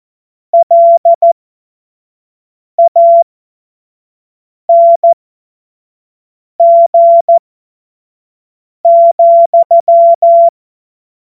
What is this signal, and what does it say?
Morse code, heard as a keyed tone, about 14 wpm, spaced as for 5 wpm, 680 Hz. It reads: LANG,
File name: Morsekurs / Witze / Witz 20 Februar 2024 25 F70